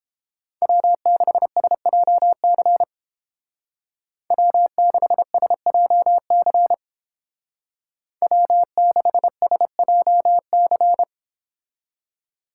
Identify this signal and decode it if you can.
Morse code, heard as a keyed tone, about 29 wpm, 700 Hz. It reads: W6SJC W6SJC W6SJC